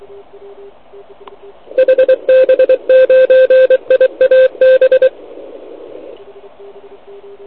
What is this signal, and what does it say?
Morse code, heard as a keyed tone, about 23 wpm, 520 Hz. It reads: HB9IAB
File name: ARCHIV / OKDX2019 / OKOM80 TS480 C